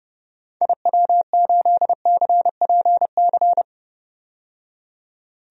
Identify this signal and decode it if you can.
Morse code, heard as a keyed tone, about 30 wpm, 700 Hz. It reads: IW8CPC
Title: DXCC-100 Call Signs - 6 Characters - 30wpm